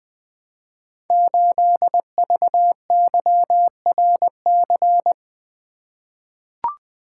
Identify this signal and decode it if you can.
Morse code, heard as a keyed tone, about 20 wpm, 700 Hz. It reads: 8VYRC
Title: Code Groups - 5 Letter:Numbers - 20wpm